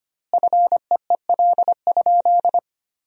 Morse code, heard as a keyed tone, about 25 wpm, 700 Hz. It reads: FEEL?
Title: 2x Word-Spacing - Sentences from Top 400 Words - 25wpm